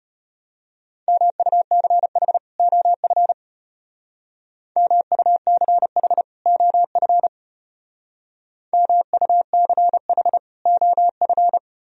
Morse code, heard as a keyed tone, about 32 wpm, 700 Hz. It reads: MUCH OF MUCH OF MUCH OF